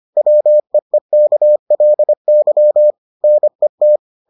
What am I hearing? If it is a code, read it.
Morse code, heard as a keyed tone, about 25 wpm, 600 Hz. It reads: WEEKLY NET